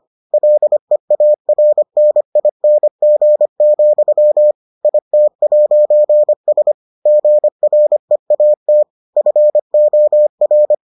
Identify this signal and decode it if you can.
Morse code, heard as a keyed tone, about 25 wpm, 600 Hz. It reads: LEARNING, IT'S GREAT FOR